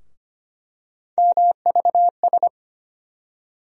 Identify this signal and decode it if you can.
Morse code, heard as a keyed tone, about 25 wpm, 700 Hz. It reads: MVS
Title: USA Call Sign Postfixes - 3 Letters 25wpm